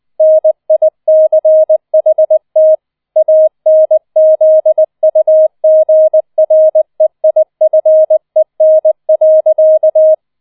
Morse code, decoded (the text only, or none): NICHT ANZUGREIFEN.